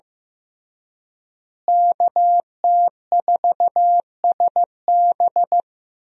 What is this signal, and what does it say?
Morse code, heard as a keyed tone, about 15 wpm, 700 Hz. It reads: KT4SB
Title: USA Call Signs - 5 Characters - Repeated 3x - 15wpm